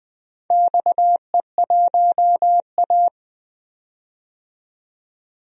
Morse code, heard as a keyed tone, about 20 wpm, 700 Hz. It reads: XE1A
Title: DXCC-100 Call Signs - 4 Characters - 20wpm